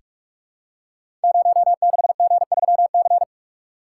Morse code, heard as a keyed tone, about 44 wpm, 700 Hz. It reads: 06G3C
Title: Code Groups - 5 Letter:Numbers - 45wpm